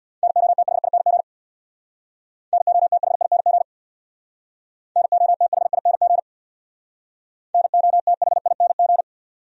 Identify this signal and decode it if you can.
Morse code, heard as a keyed tone, about 63 wpm, 700 Hz. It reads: NOTHING NOTHING NOTHING NOTHING